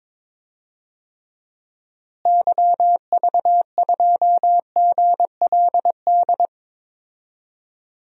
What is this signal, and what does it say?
Morse code, heard as a keyed tone, about 22 wpm, 700 Hz. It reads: YV2GLD
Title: DXCC-100 Call Signs - 6 Characters - Repeated 2x - 22wpm